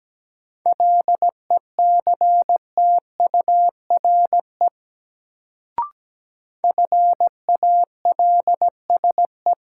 Morse code, heard as a keyed tone, about 17 wpm, 700 Hz. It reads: LECTURE FALSE
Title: Sentences from Top 1600 Words - New Word Drill Alt - 17wpm